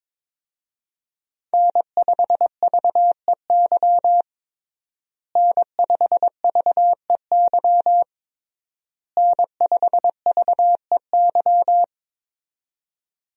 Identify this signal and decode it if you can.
Morse code, heard as a keyed tone, about 22 wpm, 700 Hz. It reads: N5VEY N5VEY N5VEY
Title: USA Call Signs - 5 Characters - Repeated 3x - 22wpm